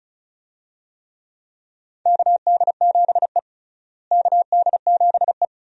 Morse code, heard as a keyed tone, about 34 wpm, 700 Hz. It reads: KD7E KD7E
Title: USA Call Signs - 4 Characters - Repeated 2x - 35wpm